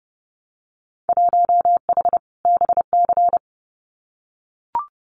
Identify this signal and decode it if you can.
Morse code, heard as a keyed tone, about 30 wpm, 700 Hz. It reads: 1H BC
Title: Winter Field Day Exchanges Encore 30wpm